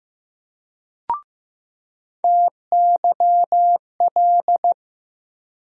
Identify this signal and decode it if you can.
Morse code, heard as a keyed tone, about 15 wpm, 700 Hz. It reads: TYL
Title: USA Call Sign Postfixes - 3 Letters 15wpm